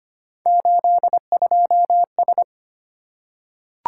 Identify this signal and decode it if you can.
Morse code, heard as a keyed tone, about 25 wpm, 700 Hz. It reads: 82S
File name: Three Letter-Number Combinations - 25wpm